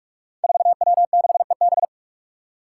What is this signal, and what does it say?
Morse code, heard as a keyed tone, about 44 wpm, 700 Hz. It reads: 4W6EB
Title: Five Challenging Letter-Number Combinations 45wpm